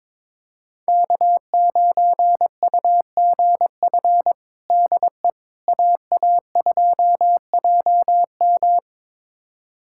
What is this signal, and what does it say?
Morse code, heard as a keyed tone, about 22 wpm, 700 Hz. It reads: K9UGF DE AA2JM
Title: USA Callsign DE Callsign - 5 Characters - 22wpm